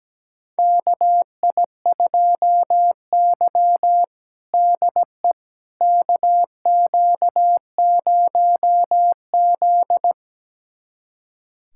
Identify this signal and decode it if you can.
Morse code, heard as a keyed tone, about 17 wpm, 700 Hz. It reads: KI2Y DE KQ0Z
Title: USA Callsign DE Callsign - 4 Characters - 17wpm